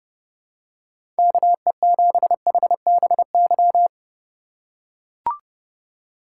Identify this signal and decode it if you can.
Morse code, heard as a keyed tone, about 30 wpm, 700 Hz. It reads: KE7HBY
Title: USA Call Signs - 6 Characters - 2x Character Spacing - 30wpm